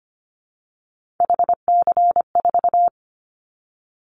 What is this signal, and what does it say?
Morse code, heard as a keyed tone, about 25 wpm, 700 Hz. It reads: HC4
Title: Callsign Prefixes for Top 100 DXCC - 25wpm